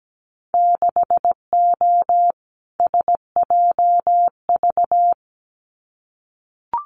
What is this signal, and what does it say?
Morse code, heard as a keyed tone, about 17 wpm, 700 Hz. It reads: 6O SJV
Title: Winter Field Day Exchanges Encore 17wpm